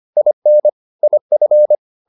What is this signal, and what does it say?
Morse code, heard as a keyed tone, about 25 wpm, 600 Hz. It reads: IN IF